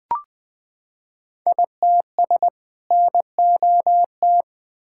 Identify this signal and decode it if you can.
Morse code, heard as a keyed tone, about 20 wpm, 700 Hz. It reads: ITS NOT
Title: Sentences with 2-Word Phrases - 20wpm